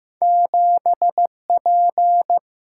Morse code, heard as a keyed tone, about 15 wpm, 700 Hz. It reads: 7 P
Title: Single-Letter Single-Number 15wpm